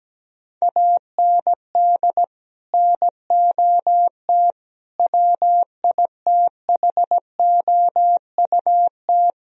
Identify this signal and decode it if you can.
Morse code, heard as a keyed tone, about 17 wpm, 700 Hz. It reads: AND NOT WITHOUT
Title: Sentences from Top 1000 Words - Review Part 3 - 17wpm